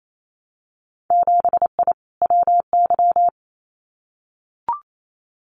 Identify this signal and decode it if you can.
Morse code, heard as a keyed tone, about 28 wpm, 700 Hz. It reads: 7I WY
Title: Winter Field Day Exchanges Encore 28wpm